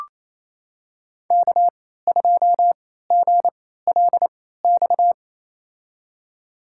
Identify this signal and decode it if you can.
Morse code, heard as a keyed tone, about 28 wpm, spaced as for 16 wpm, 700 Hz. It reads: K2GLX
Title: USA Call Signs - 3x Character Spacing - 28wpm